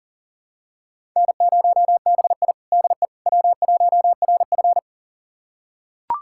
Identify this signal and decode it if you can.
Morse code, heard as a keyed tone, about 37 wpm, 700 Hz. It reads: N0BI DE W1RF
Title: USA Callsign DE Callsign - 4 Characters - 40wpm